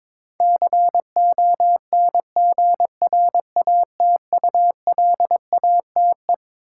Morse code, heard as a keyed tone, about 22 wpm, 700 Hz. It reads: CONGRATULATE
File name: Twelve Letter Words - 22wpm